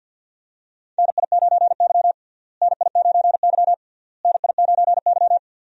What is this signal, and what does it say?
Morse code, heard as a keyed tone, about 49 wpm, 700 Hz. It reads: NI9Y NI9Y NI9Y
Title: USA Call Signs - 4 Characters - Repeated 3x - 50wpm